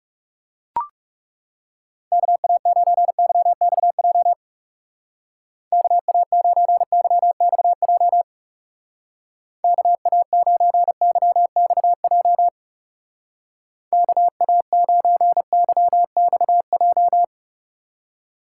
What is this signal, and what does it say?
Morse code, heard as a keyed tone, about 37 wpm, 700 Hz. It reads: KA9YXJ KA9YXJ KA9YXJ KA9YXJ
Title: Speed Racing - USA 2x3 Call Signs 30wpm